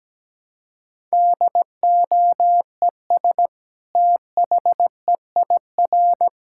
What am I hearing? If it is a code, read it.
Morse code, heard as a keyed tone, about 17 wpm, 700 Hz. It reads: DOES THEIR
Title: Sentences from Top 1000 Words - Review Part 6 - 17wpm